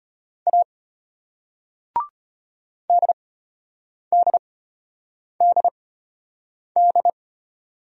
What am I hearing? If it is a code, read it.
Morse code, heard as a keyed tone, about 32 wpm, spaced as for 8 wpm, 700 Hz. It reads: A DDDD